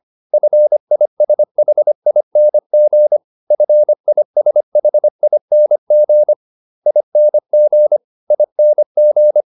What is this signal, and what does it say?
Morse code, heard as a keyed tone, about 25 wpm, 600 Hz. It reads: FISHING FISHING ING ING